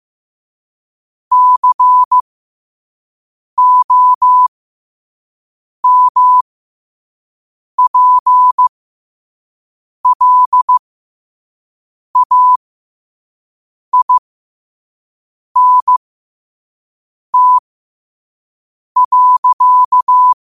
Morse code, heard as a keyed tone, about 15 wpm, spaced as for 5 wpm, 1000 Hz. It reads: COMPLAINT.